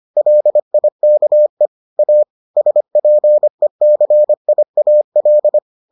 Morse code, heard as a keyed tone, about 25 wpm, 600 Hz. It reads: LIKE A SPECIAL